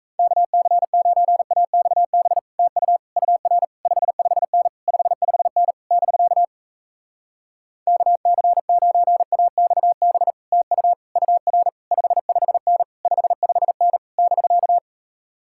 Morse code, heard as a keyed tone, about 37 wpm, 700 Hz. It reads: KC9AXB TU UR 55N 55N <BK> KC9AXB TU UR 55N 55N <BK>